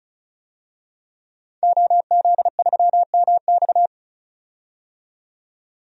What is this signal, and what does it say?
Morse code, heard as a keyed tone, about 36 wpm, 700 Hz. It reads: OZ3MX
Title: DXCC-100 Call Signs - 5 Characters - 35wpm